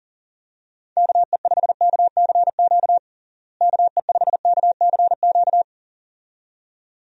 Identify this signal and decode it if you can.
Morse code, heard as a keyed tone, about 40 wpm, 700 Hz. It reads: KE5KCQ KE5KCQ
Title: USA Call Signs - 6 Characters - Repeated 2x - 40wpm